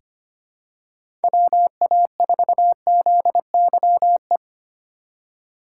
Morse code, heard as a keyed tone, about 25 wpm, 700 Hz. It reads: WA4ZYE
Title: USA Call Signs - 6 Characters - 2x Character Spacing - 25wpm